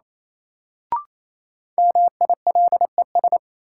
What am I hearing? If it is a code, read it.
Morse code, heard as a keyed tone, about 28 wpm, 700 Hz. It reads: MILES
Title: Sentences from Top 1000 Words - Review Part 3 - New Word Drill - 28wpm